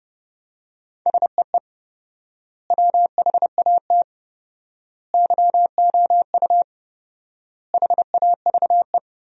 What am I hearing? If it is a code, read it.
Morse code, heard as a keyed tone, about 29 wpm, 700 Hz. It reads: SEE WHAT YOU HAVE
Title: 4x Word-Spacing - Sentences from Top 500 Words - 30wpm